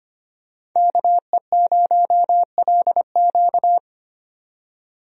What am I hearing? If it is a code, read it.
Morse code, heard as a keyed tone, about 25 wpm, 700 Hz. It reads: KE0LQ
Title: USA Call Signs - 5 Characters - 25wpm